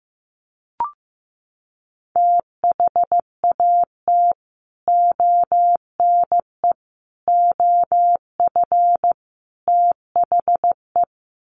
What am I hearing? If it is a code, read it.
Morse code, heard as a keyed tone, about 15 wpm, 700 Hz. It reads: THAT ONE OF THE
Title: Sets of 4 Words 15wpm